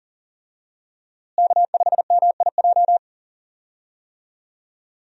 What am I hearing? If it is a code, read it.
Morse code, heard as a keyed tone, about 40 wpm, 700 Hz. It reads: K5MIJ
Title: USA Call Signs - 5 Characters - Repeated 3x - 40wpm